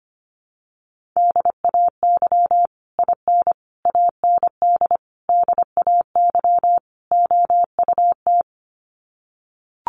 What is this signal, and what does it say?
Morse code, heard as a keyed tone, about 25 wpm, 700 Hz. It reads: DAY IN AND DAY OUT
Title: Sets of 5 Words Encore2 25wpm